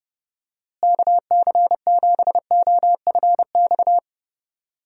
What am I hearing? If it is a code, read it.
Morse code, heard as a keyed tone, about 29 wpm, 700 Hz. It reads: KC7OFX
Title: USA Call Signs - 6 Characters - Repeated 3x - 30wpm